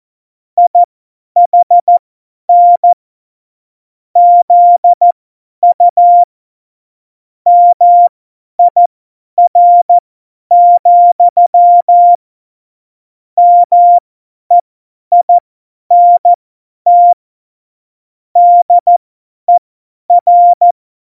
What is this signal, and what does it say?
Morse code, heard as a keyed tone, about 14 wpm, spaced as for 10 wpm, 700 Hz. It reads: IHN ZU MIR, MEINT DER
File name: Morsekurs / Witze / Witz 22. August 17, Tempo 50 F70